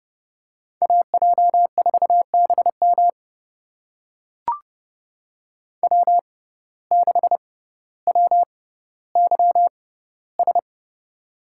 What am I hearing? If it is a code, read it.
Morse code, heard as a keyed tone, about 29 wpm, spaced as for 15 wpm, 700 Hz. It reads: AJ4BM W6WYS
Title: USA Call Signs - 5 Characters - 6x Character Spacing - 30wpm